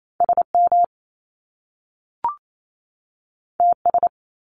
Morse code, heard as a keyed tone, about 28 wpm, 700 Hz. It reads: SM TS
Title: Top Two Letter Combinations in Words 28wpm